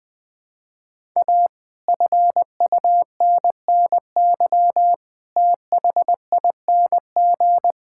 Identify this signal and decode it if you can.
Morse code, heard as a keyed tone, about 20 wpm, 700 Hz. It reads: A FUNNY THING
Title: Sentences with 8-Word Phrases - Spoken Beginning - 20wpm